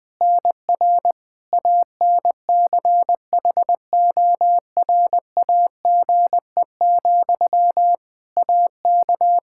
Morse code, heard as a keyed tone, about 21 wpm, 700 Hz. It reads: NR ANCHORAGE, AK
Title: Practice QSO - Exchange 1 - Beginning - 20wpm